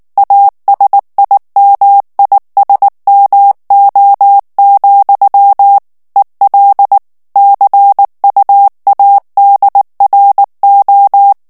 Morse code, heard as a keyed tone, about 19 wpm, 800 Hz. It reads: ASIMISMO, EL CUADRO
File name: morse mp3 / ESPAGNOL TEXTE EN CLAIR 2 VITESSE 18 MOTS MINUTES